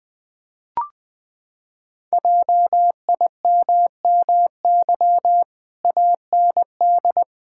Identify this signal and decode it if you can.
Morse code, heard as a keyed tone, about 20 wpm, 700 Hz. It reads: JIMMY AND
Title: Pangram Challenge v2 - 20wpm